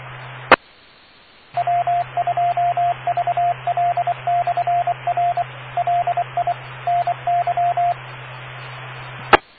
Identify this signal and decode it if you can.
Morse code, heard as a keyed tone, about 24 wpm, 700 Hz, with radio noise. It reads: E W2VL/R LI NY E